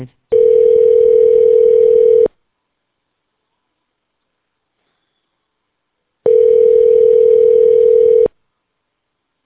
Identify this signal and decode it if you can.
Ringback tone (North America), 440 + 480 Hz.